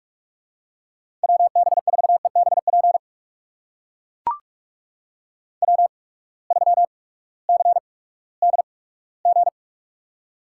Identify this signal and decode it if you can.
Morse code, heard as a keyed tone, about 43 wpm, spaced as for 20 wpm, 700 Hz. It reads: WB4EBP W3CDG
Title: USA Call Signs - 8x Character Spacing - 45wpm